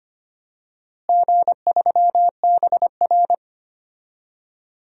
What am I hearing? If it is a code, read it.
Morse code, heard as a keyed tone, about 25 wpm, 700 Hz. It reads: G3BR
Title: DXCC-100 Call Signs - 4 Characters - Repeated 2x - 25wpm